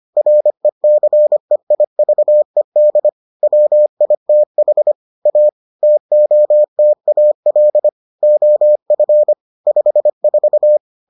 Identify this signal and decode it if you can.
Morse code, heard as a keyed tone, about 25 wpm, 600 Hz. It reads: RECEIVED WITH A TOTAL OF 54